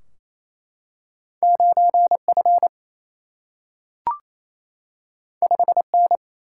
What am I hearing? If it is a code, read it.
Morse code, heard as a keyed tone, about 28 wpm, 700 Hz. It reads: 9F 5N
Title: Single-Letter Single-Number 28wpm